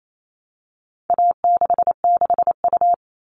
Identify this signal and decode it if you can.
Morse code, heard as a keyed tone, about 28 wpm, 700 Hz. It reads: A66U
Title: All Cut Numbers - 4-Digit 28wpm